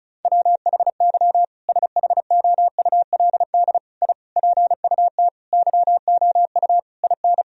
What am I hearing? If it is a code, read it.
Morse code, heard as a keyed tone, about 34 wpm, 700 Hz. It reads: WHY SHOULD I PUT YOU IN
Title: Sentences from Top 1000 Words - Review Part 4 - 35wpm